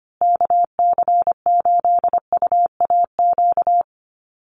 Morse code, heard as a keyed tone, about 25 wpm, 700 Hz. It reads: KC8UAQ
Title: USA Call Signs Repeated 3x 25wpm